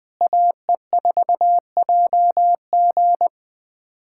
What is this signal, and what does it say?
Morse code, heard as a keyed tone, about 20 wpm, 700 Hz. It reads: AE4JG
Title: USA Callsign DE Callsign - 5 Characters - 20wpm